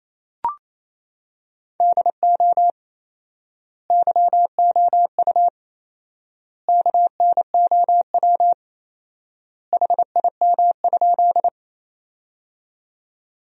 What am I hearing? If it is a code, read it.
Morse code, heard as a keyed tone, about 28 wpm, 700 Hz. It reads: DO YOU KNOW HIM?
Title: 4x Word-Spacing - Sentences from Top 100 Words - 28wpm